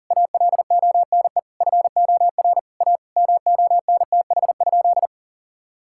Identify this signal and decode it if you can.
Morse code, heard as a keyed tone, about 40 wpm, 700 Hz. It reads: ALONE FOR A MONTH?
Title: Sentences from Top 1000 Words - Review Part 5 - 40wpm